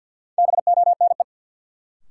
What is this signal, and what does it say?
Morse code, heard as a keyed tone, about 52 wpm, 700 Hz. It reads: BONE
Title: Sentences from Top 1600 Words - New Word Drill Alt - 50wpm